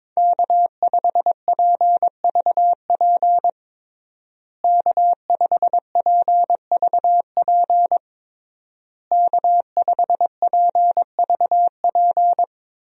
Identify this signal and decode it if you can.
Morse code, heard as a keyed tone, about 22 wpm, 700 Hz. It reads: K5PVP K5PVP K5PVP
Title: USA Call Signs - 5 Characters - Repeated 3x - 22wpm